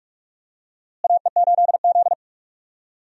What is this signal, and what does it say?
Morse code, heard as a keyed tone, about 46 wpm, 700 Hz. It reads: AE8Z
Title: USA Call Signs - 4 Characters - Repeated 3x - 45wpm